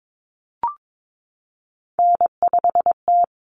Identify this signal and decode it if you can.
Morse code, heard as a keyed tone, about 22 wpm, 700 Hz. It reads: N5T
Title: Common Cut Numbers - 3-Digit 22wpm